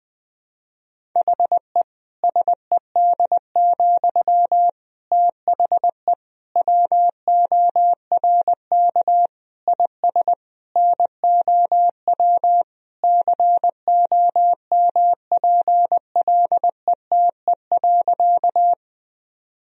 Morse code, heard as a keyed tone, about 20 wpm, 700 Hz. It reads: HE SED, THE WORK IS NOW COMPLETE.
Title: Sentences from Top 400 Words and Abbreviations 20wpm